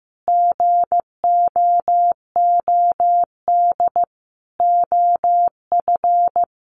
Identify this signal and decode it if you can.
Morse code, heard as a keyed tone, about 15 wpm, 700 Hz. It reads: GOOD OF
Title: Sets of 5 Words Encore2 15wpm